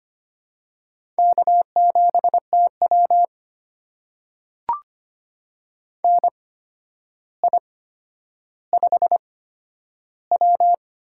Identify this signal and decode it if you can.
Morse code, heard as a keyed tone, about 25 wpm, spaced as for 11 wpm, 700 Hz. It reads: K7TW NI5W
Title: USA Call Signs - 4 Characters - 8x Character Spacing - 25wpm